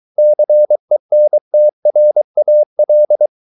Morse code, heard as a keyed tone, about 23 wpm, 600 Hz. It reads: CENTRAL